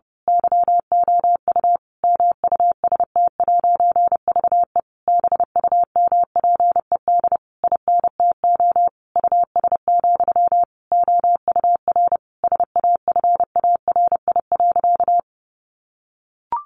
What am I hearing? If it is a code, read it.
Morse code, heard as a keyed tone, about 28 wpm, 700 Hz. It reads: YOU MUST'VE BUMPED INTO US, OUR SAFARI.
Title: A Sound Of Thunder 30wpm